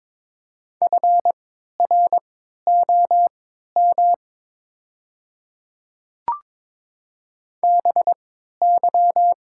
Morse code, heard as a keyed tone, about 23 wpm, spaced as for 13 wpm, 700 Hz. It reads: FROM BY
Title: 3x Character Spacing - Top 100 Words - 22wpm